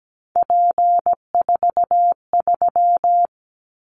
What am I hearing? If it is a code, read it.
Morse code, heard as a keyed tone, about 17 wpm, 700 Hz. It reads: P43
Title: Callsign Prefixes for Top 100 DXCC - 17wpm